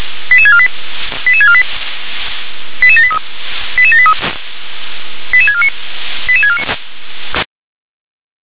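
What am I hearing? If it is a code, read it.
Five-tone selective call (ZVEI), heard as quick runs of five tones, tones of about 70 ms. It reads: 88648 88648 88783 88783 88649 88649
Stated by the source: ZVEI five-tone selective call 88646, received on SDR.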